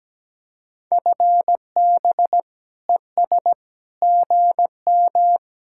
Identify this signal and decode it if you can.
Morse code, heard as a keyed tone, about 19 wpm, 700 Hz. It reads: FB ES GM
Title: Speed-Racing Common QSO Phrases - 17wpm